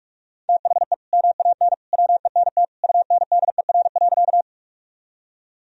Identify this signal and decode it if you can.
Morse code, heard as a keyed tone, about 41 wpm, 700 Hz. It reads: THE MAN WENT UNDER.
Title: Sentences from Top 1000 Words - Review Part 1 - 45wpm